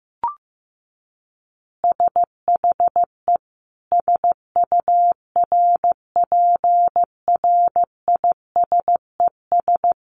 Morse code, heard as a keyed tone, about 15 wpm, 700 Hz. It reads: SHE SURPRISES
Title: Additional Easy Phrases 15wpm